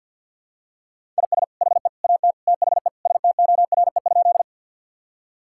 Morse code, heard as a keyed tone, about 46 wpm, 700 Hz. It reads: IS HE AT THE STORE?
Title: Sentences from Top 1000 Words - Review Part 5 - 50wpm